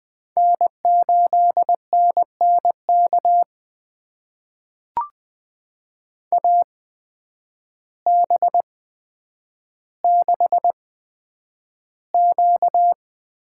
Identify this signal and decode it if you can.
Morse code, heard as a keyed tone, about 20 wpm, spaced as for 9 wpm, 700 Hz. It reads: N8NNK AB6Q
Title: USA Call Signs - 8x Character Spacing - 20wpm